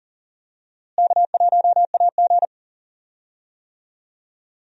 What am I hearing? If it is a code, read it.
Morse code, heard as a keyed tone, about 40 wpm, 700 Hz. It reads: K1AG
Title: USA Call Signs - 4 Characters - Repeated 3x - 40wpm